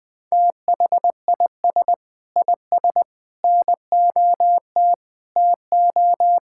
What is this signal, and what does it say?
Morse code, heard as a keyed tone, about 19 wpm, 700 Hz. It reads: THIS IS NOT TO
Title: Sentences with 6-Word Phrases - Spoken Beginning - 20wpm